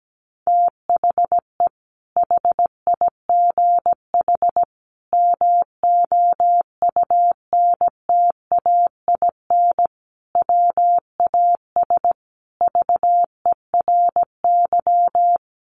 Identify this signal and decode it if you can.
Morse code, heard as a keyed tone, about 17 wpm, 700 Hz. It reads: THE HIGH MOUNTAIN WAS VERY
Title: Sentences from Top 1300 Words 17wpm